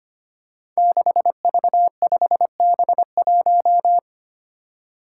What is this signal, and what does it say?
Morse code, heard as a keyed tone, about 25 wpm, 700 Hz. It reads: 6V5B1
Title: Five Challenging Letter-Number Combinations 25wpm